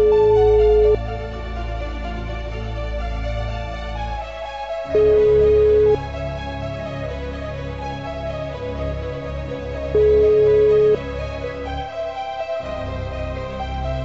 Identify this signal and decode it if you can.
Ringback tone (Europe, 425 Hz).